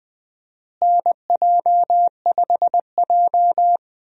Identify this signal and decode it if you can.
Morse code, heard as a keyed tone, about 20 wpm, 700 Hz. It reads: NJ5J
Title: USA Call Signs - 4 Characters - Repeated 3x - 20wpm